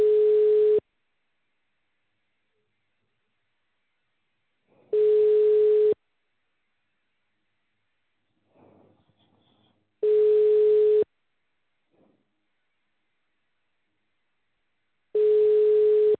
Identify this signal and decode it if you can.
Ringback tone (Europe, 425 Hz).